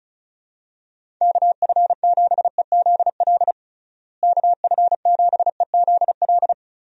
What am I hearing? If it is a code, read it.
Morse code, heard as a keyed tone, about 36 wpm, 700 Hz. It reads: KF7EZL KF7EZL
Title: USA Call Signs - 6 Characters - Repeated 2x - 35wpm